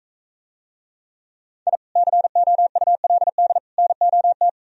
Morse code, heard as a keyed tone, about 43 wpm, 700 Hz. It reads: I COULD NOT